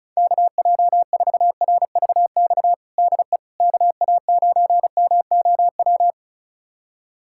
Morse code, heard as a keyed tone, about 34 wpm, 700 Hz. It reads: KJ4RVX DE KA9MOW
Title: USA Callsign DE Callsign - 6 Characters - 35wpm